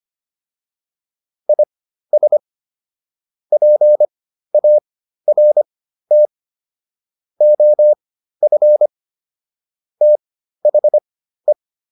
Morse code, heard as a keyed tone, about 25 wpm, spaced as for 13 wpm, 600 Hz. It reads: IS PART OF THE